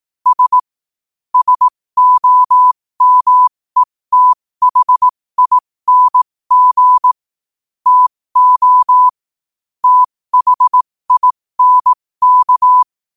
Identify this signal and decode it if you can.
Morse code, heard as a keyed tone, about 18 wpm, spaced as for 15 wpm, 1000 Hz. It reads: S SOMETHING TO THINK
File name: qotdfiles / 15wpm